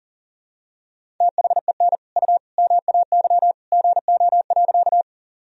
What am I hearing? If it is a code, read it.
Morse code, heard as a keyed tone, about 37 wpm, 700 Hz. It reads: THEN U MAY GO.
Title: Sentences from Top 100 Words and Abbreviations 40wpm